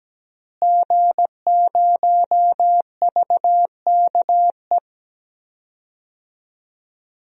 Morse code, heard as a keyed tone, about 17 wpm, 700 Hz. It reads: G0VKE
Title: DXCC-100 Call Signs - 5 Characters - Repeated 3x - 17wpm